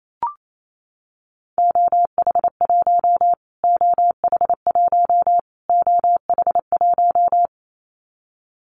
Morse code, heard as a keyed tone, about 28 wpm, 700 Hz. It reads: OH1 OH1 OH1
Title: Callsign Prefixes for Top 25 DXCC - Repeated 3x - 28wpm